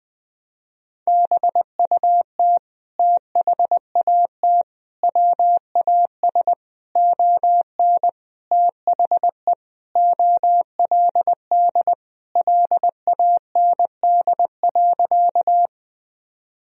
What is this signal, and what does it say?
Morse code, heard as a keyed tone, about 20 wpm, 700 Hz. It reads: BUT THAT WAS ON THE OLD LAND.
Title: Sentences from Top 1000 Words - Review Part 1 - 20wpm